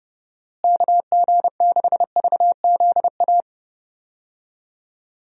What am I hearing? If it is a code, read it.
Morse code, heard as a keyed tone, about 30 wpm, 700 Hz. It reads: KG6VZA
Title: USA Call Signs - 6 Characters - Repeated 3x - 30wpm